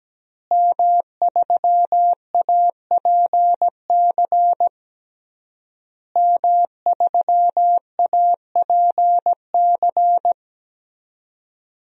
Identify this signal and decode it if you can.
Morse code, heard as a keyed tone, about 17 wpm, 700 Hz. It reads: M3APC M3APC
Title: DXCC-100 Call Signs - 5 Characters - Repeated 3x - 17wpm